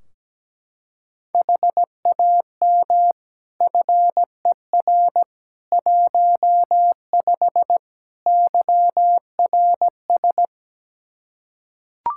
Morse code, heard as a keyed tone, about 17 wpm, 700 Hz. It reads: HAM FER 15 YRS
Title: Practice QSO - Exchange 3 - More Info - 17wpm